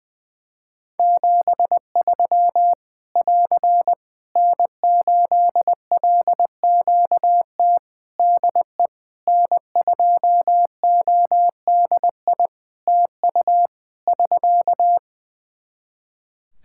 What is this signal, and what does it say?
Morse code, heard as a keyed tone, about 20 wpm, 700 Hz. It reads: 73 <AR> N8LQT DE N2ODI TU <SK>
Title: K1USN SST QSO Practice 20wpm